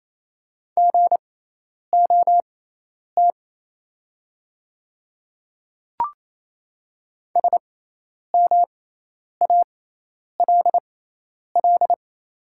Morse code, heard as a keyed tone, about 27 wpm, spaced as for 10 wpm, 700 Hz. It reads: GOT SMALL
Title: 6x Character Spacing - Top 500 Words - 28wpm